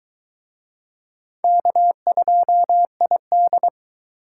Morse code, heard as a keyed tone, about 22 wpm, 700 Hz. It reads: K2ID